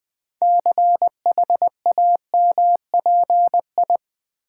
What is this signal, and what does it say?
Morse code, heard as a keyed tone, about 20 wpm, 700 Hz. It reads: CHAMPI
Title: Build A Word - 6 Letters At A Time - 12-Letter Words - 20wpm